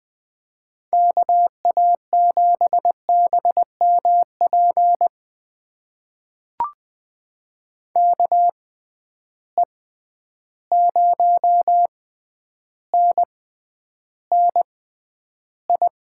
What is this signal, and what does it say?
Morse code, heard as a keyed tone, about 20 wpm, spaced as for 10 wpm, 700 Hz. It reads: KA7BMP KE0NNI
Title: USA Call Signs - 6 Characters - 6x Character Spacing - 20wpm